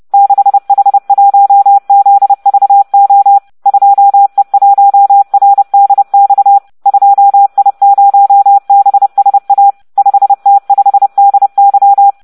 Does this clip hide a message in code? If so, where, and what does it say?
Yes: Morse code from 0.1 to 12.1 s, heard as a keyed tone, about 30 wpm, 800 Hz. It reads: 6H1ZVO 2E1RDX 2I0BSA 5T5DY